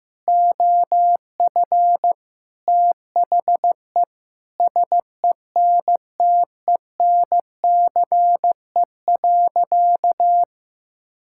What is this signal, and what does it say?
Morse code, heard as a keyed tone, about 15 wpm, 700 Hz. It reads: OF THE SENTENCE.
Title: Sentences from Top 1000 Words - Review Part 6 - 15wpm